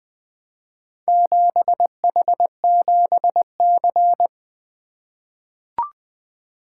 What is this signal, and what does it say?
Morse code, heard as a keyed tone, about 20 wpm, 700 Hz. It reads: 7H7C
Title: Four Letter-Number Combinations - 20wpm